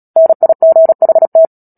Morse code, heard as a keyed tone, about 37 wpm, 650 Hz. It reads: NIGHT
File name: Book Two / 10-Chapter 10 - Epilogue - 36 WPM